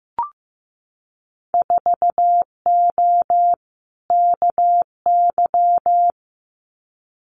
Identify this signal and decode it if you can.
Morse code, heard as a keyed tone, about 15 wpm, 700 Hz. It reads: 4O KY
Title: Winter Field Day Exchanges Encore 15wpm